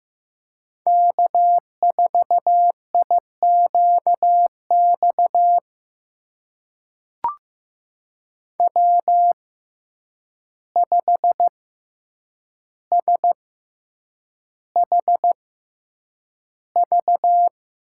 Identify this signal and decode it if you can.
Morse code, heard as a keyed tone, about 15 wpm, spaced as for 8 wpm, 700 Hz. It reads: K4IQX W5SHV
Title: USA Call Signs - 6x Character Spacing - 15wpm